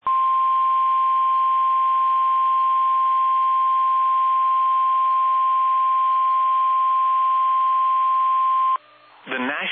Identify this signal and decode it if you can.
Emergency Alert System (SAME) tones (the 1050 Hz attention tone), with speech. It is heard from 0.0 to 8.8 s.
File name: eas / KBBX / KBBX